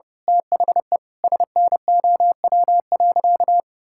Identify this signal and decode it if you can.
Morse code, heard as a keyed tone, about 32 wpm, 700 Hz. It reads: THE SNOW.
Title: Sentences from Top 1000 Words - Review Part 5 - 30wpm